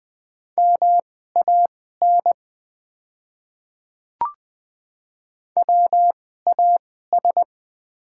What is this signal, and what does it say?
Morse code, heard as a keyed tone, about 21 wpm, spaced as for 15 wpm, 700 Hz. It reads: MAN WAS
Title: 2x Character Spacing - Top 200 Words - 20wpm